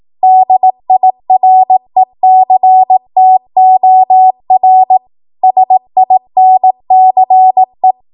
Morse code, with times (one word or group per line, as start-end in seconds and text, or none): 0.2-5.0 DIRECTOR
5.4-7.9 SINCE